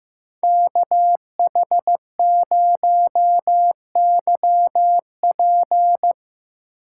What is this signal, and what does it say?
Morse code, heard as a keyed tone, about 15 wpm, 700 Hz. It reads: KH0YP
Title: USA Call Signs - 5 Characters - 4x Character Spacing - 15wpm